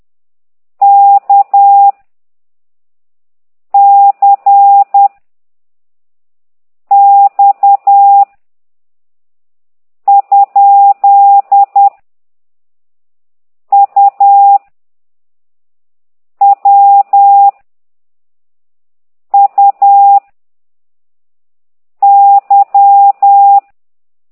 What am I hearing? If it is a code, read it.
Morse code, heard as a keyed tone, about 10 wpm, 800 Hz. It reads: K C X ? U W U Y
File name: ct2idw / exerc9